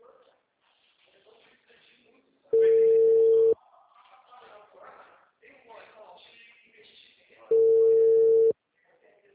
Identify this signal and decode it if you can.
Ringback tone (China, 450 Hz), with speech.